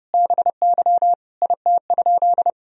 Morse code, heard as a keyed tone, about 30 wpm, 700 Hz. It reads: BY IT?
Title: Sentences from Top 100 Words and Abbreviations 30wpm